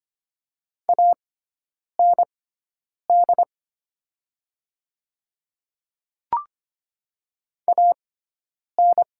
Morse code, heard as a keyed tone, about 25 wpm, spaced as for 9 wpm, 700 Hz. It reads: AND AN
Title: 6x Character Spacing - Top 100 Words - 25wpm